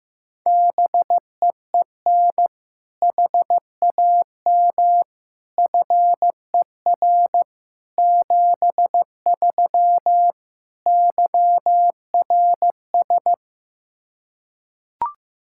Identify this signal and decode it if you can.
Morse code, heard as a keyed tone, about 16 wpm, 700 Hz. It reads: BEEN HAM FER 73 YRS
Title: Practice QSO - Exchange 3 - More Info - 15wpm